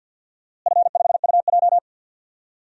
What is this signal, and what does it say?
Morse code, heard as a keyed tone, about 48 wpm, 700 Hz. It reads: F5UJ
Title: DXCC-100 Call Signs - 4 Characters - Repeated 2x - 50wpm